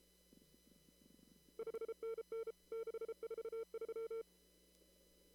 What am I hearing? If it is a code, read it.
Morse code, heard as a keyed tone, about 32 wpm, 450 Hz. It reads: 5NN 643
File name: CW / kt5j wpx cw-op-k5pi